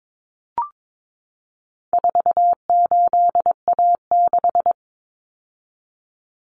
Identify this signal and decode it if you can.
Morse code, heard as a keyed tone, about 22 wpm, 700 Hz. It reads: 48A6
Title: Common Cut Numbers - 4-Digit 22wpm